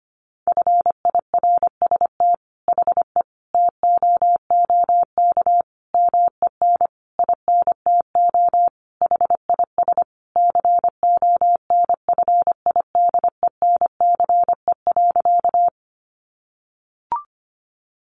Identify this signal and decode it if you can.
Morse code, heard as a keyed tone, about 25 wpm, 700 Hz. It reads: FIRST HE TOOK MEN INTO HIS CONFIDENCE.
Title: Sentences from Top 1500 Words 25wpm